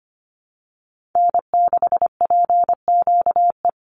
Morse code, heard as a keyed tone, about 25 wpm, 700 Hz. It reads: N6PQE
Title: USA Call Signs 25wpm